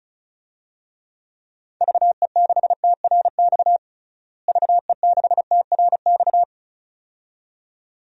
Morse code, heard as a keyed tone, about 36 wpm, 700 Hz. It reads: VE6TRX VE6TRX
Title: DXCC-100 Call Signs - 6 Characters - Repeated 2x - 35wpm